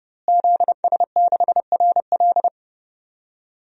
Morse code, heard as a keyed tone, about 30 wpm, 700 Hz. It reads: ZS6RL
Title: DXCC-100 Call Signs - 5 Characters - Repeated 3x - 30wpm